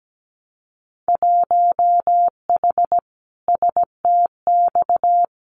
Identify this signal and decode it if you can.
Morse code, heard as a keyed tone, about 17 wpm, 700 Hz. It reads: 1H STX
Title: Winter Field Day Exchanges Encore 17wpm